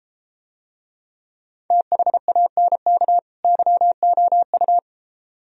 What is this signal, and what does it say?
Morse code, heard as a keyed tone, about 32 wpm, 700 Hz. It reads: THANK YOU